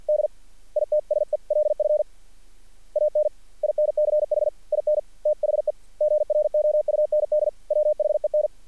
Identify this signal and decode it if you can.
Morse code, heard as a keyed tone, about 48 wpm, 590 Hz. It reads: D ITSELF AN INCH IN THE GROUND WHEN
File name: archived / CRSnet